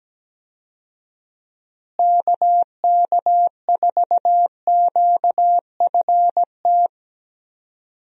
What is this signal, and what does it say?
Morse code, heard as a keyed tone, about 17 wpm, 700 Hz. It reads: KK4QFT
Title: USA Call Signs - 6 Characters - Repeated 3x - 17wpm